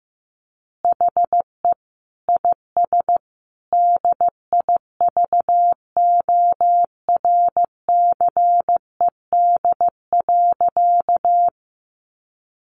Morse code, heard as a keyed tone, about 15 wpm, 700 Hz. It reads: HE IS DIVORCED.
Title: Additional Easy Phrases 15wpm